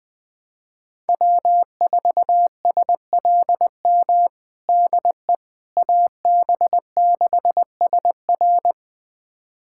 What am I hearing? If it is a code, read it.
Morse code, heard as a keyed tone, about 20 wpm, 700 Hz. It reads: W4SLM DE AB6SR
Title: USA Callsign DE Callsign - 5 Characters - 20wpm